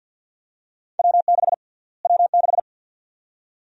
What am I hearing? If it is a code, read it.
Morse code, heard as a keyed tone, about 51 wpm, 700 Hz. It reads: W6 W6
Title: USA Call Sign Prefixes - Repeated 2x - 50wpm